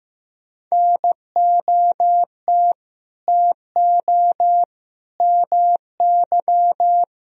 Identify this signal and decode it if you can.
Morse code, heard as a keyed tone, about 15 wpm, 700 Hz. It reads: NOT TO MY
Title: Sentences from Top 1000 Words - Review Part 3 - 15wpm